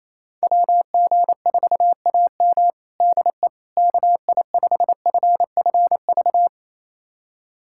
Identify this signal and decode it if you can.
Morse code, heard as a keyed tone, about 27 wpm, 700 Hz. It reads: WG4AM DE KI5FFV
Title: USA Callsign DE Callsign 28wpm